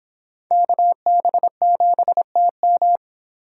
Morse code, heard as a keyed tone, about 25 wpm, 700 Hz. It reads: KB7TM